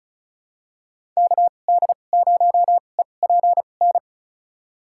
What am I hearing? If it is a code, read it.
Morse code, heard as a keyed tone, about 36 wpm, spaced as for 25 wpm, 700 Hz. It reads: KD0EPN
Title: USA Call Signs - 6 Characters - 2x Character Spacing - 35wpm